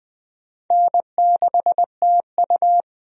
Morse code, heard as a keyed tone, about 20 wpm, 700 Hz. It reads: N6TU
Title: USA Call Signs - 4 Characters - 20wpm